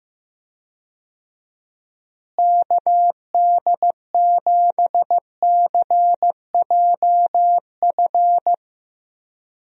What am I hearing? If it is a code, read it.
Morse code, heard as a keyed tone, about 15 wpm, 700 Hz. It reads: KD7CJF
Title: USA Call Signs - 6 Characters - Repeated 3x - 15wpm